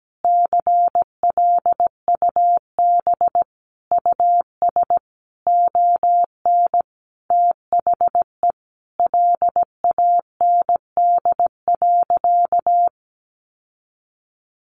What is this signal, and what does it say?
Morse code, heard as a keyed tone, about 17 wpm, 700 Hz. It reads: CLUB US ON THE LAND.
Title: Sentences from Top 1200 Words 17wpm